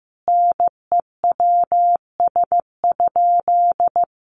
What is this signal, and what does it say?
Morse code, heard as a keyed tone, about 15 wpm, 700 Hz. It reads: NEWS?
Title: Sentences from Top 1200 Words 15wpm